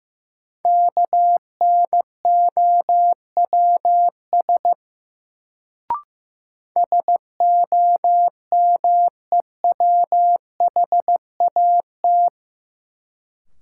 Morse code, heard as a keyed tone, about 15 wpm, 700 Hz. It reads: KNOWS SOMEWHAT
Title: Sentences from Top 1000 Words - Review Part 1 - New Word Drill - 15wpm